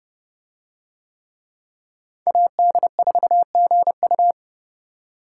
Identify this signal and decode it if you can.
Morse code, heard as a keyed tone, about 29 wpm, 700 Hz. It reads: AD4GU